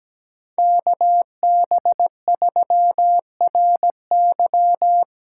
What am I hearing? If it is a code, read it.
Morse code, heard as a keyed tone, about 20 wpm, 700 Hz. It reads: KB3RY
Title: Speed Racing - USA Call Signs 17wpm